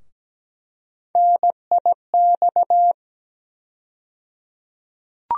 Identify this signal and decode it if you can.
Morse code, heard as a keyed tone, about 17 wpm, 700 Hz. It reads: NIX
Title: USA Call Sign Postfixes - 3 Letters 17wpm